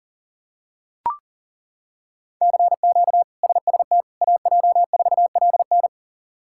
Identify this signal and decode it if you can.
Morse code, heard as a keyed tone, about 40 wpm, 700 Hz. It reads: CQ SST AJ4LN
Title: K1USN SST QSO Practice - 40wpm